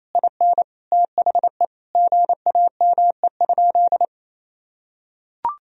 Morse code, heard as a keyed tone, about 28 wpm, 700 Hz. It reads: IN THE GAME?
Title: Sentences from Top 1000 Words - Review Part 4 - 28wpm